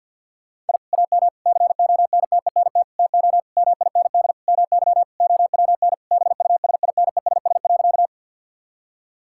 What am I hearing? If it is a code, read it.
Morse code, heard as a keyed tone, about 50 wpm, 700 Hz. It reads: I AM CONTENT TO MIND MY OWN BUSINESS.
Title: Sentences from Top 1000 Words - Review Part 6 - 50wpm